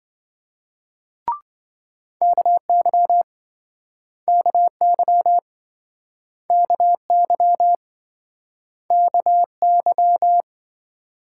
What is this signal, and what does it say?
Morse code, heard as a keyed tone, about 29 wpm, 700 Hz. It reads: KY KY KY KY